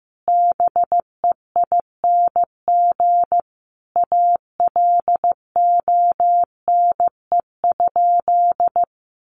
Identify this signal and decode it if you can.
Morse code, heard as a keyed tone, about 15 wpm, 700 Hz. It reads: BEING ALONE?
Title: Sentences from Top 1100 Words 15wpm